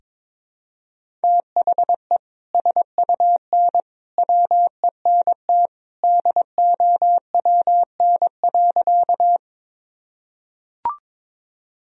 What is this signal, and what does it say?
Morse code, heard as a keyed tone, about 22 wpm, 700 Hz. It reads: THE SUN WENT DOWN.
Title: Sentences from Top 1000 Words - Review Part 1 - 22wpm